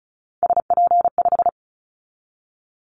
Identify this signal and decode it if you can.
Morse code, heard as a keyed tone, about 36 wpm, 700 Hz. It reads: SP5
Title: Callsign Prefixes for Top 25 DXCC - 35wpm